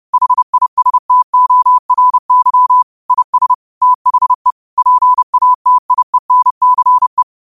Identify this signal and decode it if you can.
Morse code, heard as a keyed tone, about 30 wpm, 1000 Hz. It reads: HISTORY IS THE PATIENCE
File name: qotdfiles / 30wpm